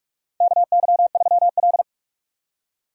Morse code, heard as a keyed tone, about 46 wpm, 700 Hz. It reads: KY3L
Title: USA Call Signs - 4 Characters - 6x Character Spacing - 45wpm